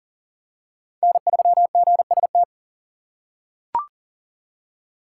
Morse code, heard as a keyed tone, about 40 wpm, 700 Hz. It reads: N3GST
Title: USA Call Signs - 4x Character Spacing - 40wpm